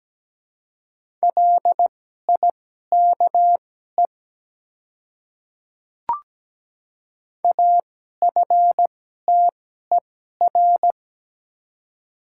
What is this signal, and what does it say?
Morse code, heard as a keyed tone, about 17 wpm, spaced as for 12 wpm, 700 Hz. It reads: LIKE AFTER
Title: 2x Character Spacing - Top 300 Words - 17wpm